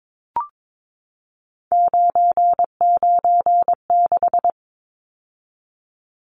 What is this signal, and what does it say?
Morse code, heard as a keyed tone, about 22 wpm, 700 Hz. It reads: 996
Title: Numbers - 3-Digit 22wpm